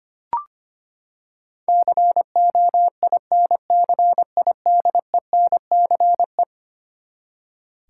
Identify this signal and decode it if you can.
Morse code, heard as a keyed tone, about 25 wpm, 700 Hz. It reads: COINCIDENCE
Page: morse-code-ninja-practice-25wpm